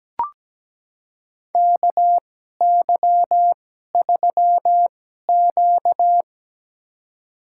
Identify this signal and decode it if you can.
Morse code, heard as a keyed tone, about 17 wpm, spaced as for 12 wpm, 700 Hz. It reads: KY3Q
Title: USA Call Signs - 2x Character Spacing - 17wpm